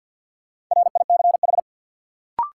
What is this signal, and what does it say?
Morse code, heard as a keyed tone, about 52 wpm, 700 Hz. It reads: RICH
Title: Sentences from Top 1100 Words - New Word Drill - 50wpm